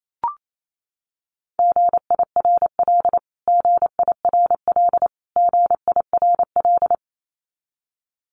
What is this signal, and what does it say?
Morse code, heard as a keyed tone, about 28 wpm, 700 Hz. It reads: GIRL GIRL GIRL
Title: Top 500 Words Repeated 28wpm